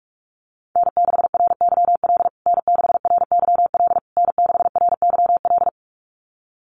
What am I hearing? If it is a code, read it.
Morse code, heard as a keyed tone, about 44 wpm, 700 Hz. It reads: N6RYL N6RYL N6RYL
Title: USA Call Signs Repeated 3x 45wpm